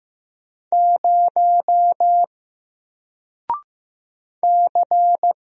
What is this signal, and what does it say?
Morse code, heard as a keyed tone, about 15 wpm, 700 Hz. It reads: 0 C
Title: Single Character (Letter:Number:Punctuation:Prosign) Alt 15wpm